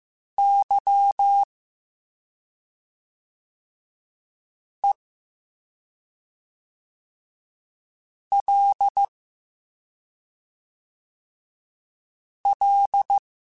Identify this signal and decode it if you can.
Morse code, heard as a keyed tone, about 15 wpm, spaced as for 2 wpm, 800 Hz. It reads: YELL